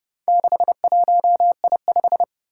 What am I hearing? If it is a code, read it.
Morse code, heard as a keyed tone, about 30 wpm, 700 Hz. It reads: 61I5
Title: Four Challenging Letter-Number Combinations 30wpm